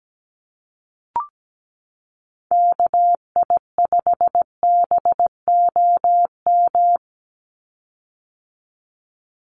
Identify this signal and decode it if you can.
Morse code, heard as a keyed tone, about 17 wpm, 700 Hz. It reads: KI5BOM
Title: USA Call Signs 17wpm